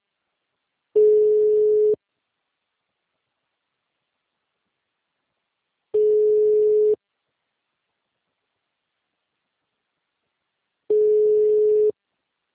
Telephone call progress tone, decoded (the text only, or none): ringback tone